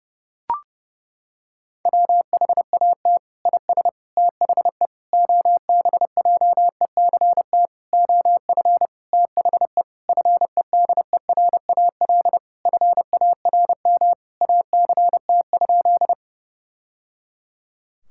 Morse code, heard as a keyed tone, about 30 wpm, 700 Hz. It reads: WHAT IS THE OBJECT OF THE FEDERAL FARM ACT?
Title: Sentences from Top 1000 Words - Review Part 5 - 30wpm